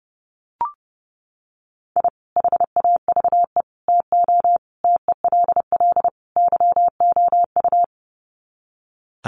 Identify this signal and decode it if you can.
Morse code, heard as a keyed tone, about 30 wpm, 700 Hz. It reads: I HAVE TO TELL YOU
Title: Sets of 5 Words 30wpm